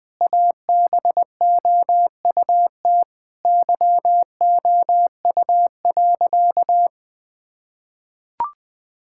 Morse code, heard as a keyed tone, about 20 wpm, 700 Hz. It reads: ABOUT YOU.
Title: Sentences with 7-Word Phrases - Spoken Beginning - 20wpm